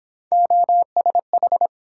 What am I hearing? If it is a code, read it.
Morse code, heard as a keyed tone, about 27 wpm, 700 Hz. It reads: OSH
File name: Speed-Racing - Three Letter-Number Combinations - 22wpm